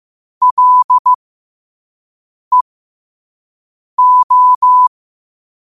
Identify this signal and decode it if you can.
Morse code, heard as a keyed tone, about 15 wpm, spaced as for 5 wpm, 1000 Hz. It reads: LEO